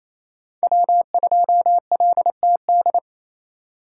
Morse code, heard as a keyed tone, about 28 wpm, 700 Hz. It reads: W2LTD
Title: USA Call Signs - 5 Characters - 4x Character Spacing - 28wpm